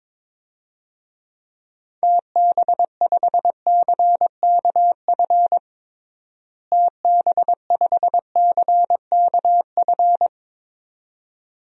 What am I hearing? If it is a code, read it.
Morse code, heard as a keyed tone, about 21 wpm, 700 Hz. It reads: TB5CKF TB5CKF
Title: DXCC-100 Call Signs - 6 Characters - Repeated 2x - 22wpm